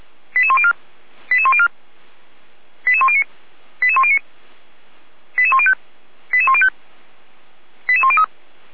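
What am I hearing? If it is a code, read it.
Five-tone selective call (ZVEI), heard as quick runs of five tones, tones of about 70 ms. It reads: 88104 88104 88109 88109 88105 88105 88103
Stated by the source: ZVEI five-tone call 88102 received by SDR.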